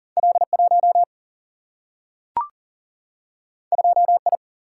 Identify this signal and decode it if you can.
Morse code, heard as a keyed tone, about 40 wpm, 700 Hz. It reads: L 1 2 I
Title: Single-Letter Single-Number 40wpm